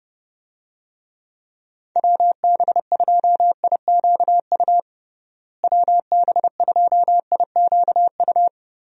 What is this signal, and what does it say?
Morse code, heard as a keyed tone, about 32 wpm, 700 Hz. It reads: WB2IQU WB2IQU
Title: USA Call Signs - 6 Characters - Repeated 3x - 30wpm